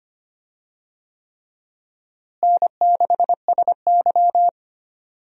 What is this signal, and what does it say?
Morse code, heard as a keyed tone, about 25 wpm, 700 Hz. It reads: N6SY
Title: USA Call Signs - 4 Characters - Repeated 3x - 25wpm